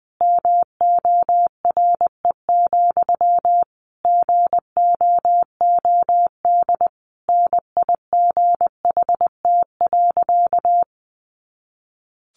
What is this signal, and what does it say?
Morse code, heard as a keyed tone, about 20 wpm, 700 Hz. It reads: MORE, GOOD NIGHT.
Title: Sentences from Top 300 Words 20wpm